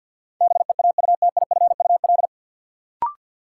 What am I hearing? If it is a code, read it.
Morse code, heard as a keyed tone, about 49 wpm, 700 Hz. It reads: BEAUTIFUL
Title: Sentences from Top 900 Words - New Word Drill - 50wpm